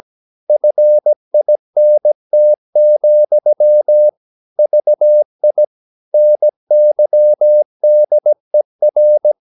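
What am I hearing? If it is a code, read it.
Morse code, heard as a keyed tone, about 17 wpm, 600 Hz. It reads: FINT, VI NYDER
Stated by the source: Slow (QRS) Morse practice at 17 wpm, from CW med Gnister.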